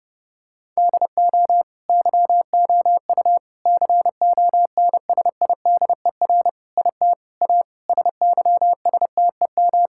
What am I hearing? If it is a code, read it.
Morse code, heard as a keyed tone, about 32 wpm, 700 Hz. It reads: DO YOU CONSIDER IT A SYSTEM
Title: Sentences from Top 1000 Words - Review Part 6 - 30wpm